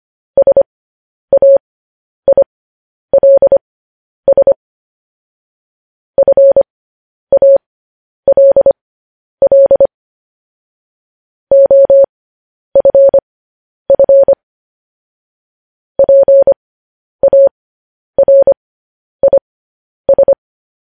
Morse code, heard as a keyed tone, about 25 wpm, spaced as for 10 wpm, 550 Hz. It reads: SAILS FALL OFF PARIS
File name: podcast / News headlines-1x-25x10